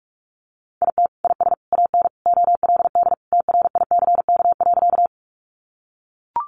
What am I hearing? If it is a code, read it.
Morse code, heard as a keyed tone, about 44 wpm, 700 Hz. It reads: IT IS AN OLD TRICK.
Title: Sentences from Top 1500 Words 45wpm